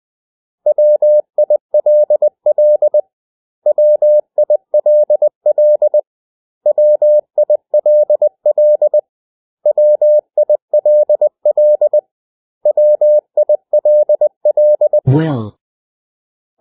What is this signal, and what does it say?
Morse code, heard as a keyed tone, about 20 wpm, 600 Hz. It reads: WILL WILL WILL WILL WILL
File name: cw / 100 Common English Words